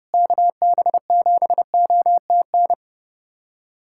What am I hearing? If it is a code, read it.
Morse code, heard as a keyed tone, about 32 wpm, 700 Hz. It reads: KB7OTN
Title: USA Callsign DE Callsign 30wpm